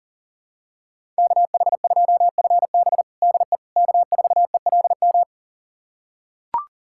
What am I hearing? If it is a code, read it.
Morse code, heard as a keyed tone, about 40 wpm, 700 Hz. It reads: KH2FB DE K4ELM
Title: USA Callsign DE Callsign - 5 Characters - 40wpm